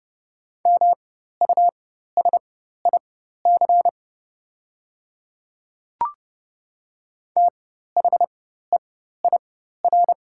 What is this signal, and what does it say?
Morse code, heard as a keyed tone, about 32 wpm, spaced as for 14 wpm, 700 Hz. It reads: MUSIC THEIR
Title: 4x Character Spacing - Top 300 Words - 30wpm